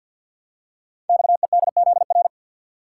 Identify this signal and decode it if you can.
Morse code, heard as a keyed tone, about 52 wpm, 700 Hz. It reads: XEDZR
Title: Code Groups - 5 Letters - 50wpm